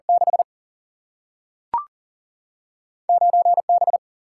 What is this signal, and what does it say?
Morse code, heard as a keyed tone, about 40 wpm, 700 Hz. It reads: 6 9 B
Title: Single-Letter Single-Number 40wpm